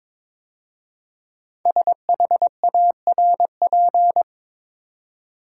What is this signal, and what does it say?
Morse code, heard as a keyed tone, about 22 wpm, 700 Hz. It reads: SHARP
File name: 5 Letter Words - First 3 Letters Given - 22wpm